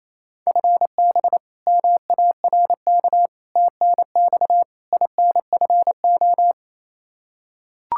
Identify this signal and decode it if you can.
Morse code, heard as a keyed tone, about 28 wpm, 700 Hz. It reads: FB MARK TNX INFO
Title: K1USN SST QSO Practice 28wpm